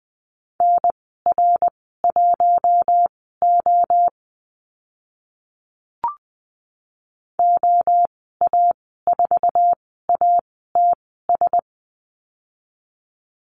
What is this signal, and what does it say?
Morse code, heard as a keyed tone, about 20 wpm, spaced as for 14 wpm, 700 Hz. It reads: NR1O OA4ATS
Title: 082 Lesson 14g - Number 2 - Call Signs 20wpm